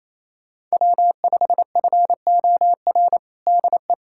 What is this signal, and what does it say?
Morse code, heard as a keyed tone, about 28 wpm, 700 Hz. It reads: W5FOR DE
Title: USA Callsign DE Callsign - 5 Characters - 28wpm